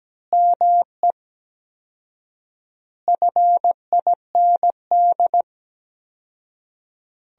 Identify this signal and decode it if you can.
Morse code, heard as a keyed tone, about 17 wpm, 700 Hz. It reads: ME FIND
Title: 4x Word-Spacing - Sentences from Top 200 Words - 17wpm